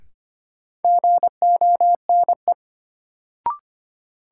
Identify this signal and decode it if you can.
Morse code, heard as a keyed tone, about 25 wpm, 700 Hz. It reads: GONE
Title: Sentences from Top 1200 Words - New Word Drill - 25wpm